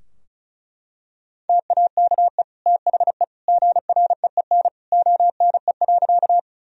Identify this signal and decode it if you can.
Morse code, heard as a keyed tone, about 34 wpm, 700 Hz. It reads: TAKE THE GREEN ONE.
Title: Sentences from Top 1000 Words - Review Part 2 - 35wpm